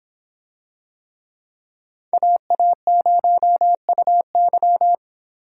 Morse code, heard as a keyed tone, about 24 wpm, 700 Hz. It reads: AA0UY